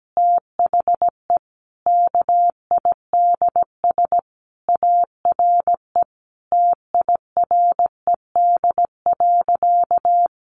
Morse code, heard as a keyed tone, about 17 wpm, 700 Hz. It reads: THE KIDS ARE TIRED.
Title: Easy Phrases 17wpm